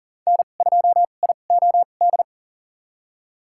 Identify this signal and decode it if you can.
Morse code, heard as a keyed tone, about 40 wpm, spaced as for 29 wpm, 700 Hz. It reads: N2IOD
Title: USA Call Signs - 5 Characters - 2x Character Spacing - 40wpm